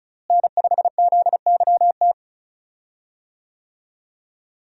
Morse code, heard as a keyed tone, about 34 wpm, 700 Hz. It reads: N5ZYT
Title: USA Call Signs - 5 Characters - Repeated 2x - 35wpm